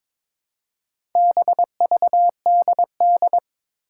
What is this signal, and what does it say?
Morse code, heard as a keyed tone, about 22 wpm, 700 Hz. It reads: BVDD
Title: Four Challenging Letter-Number Combinations 22wpm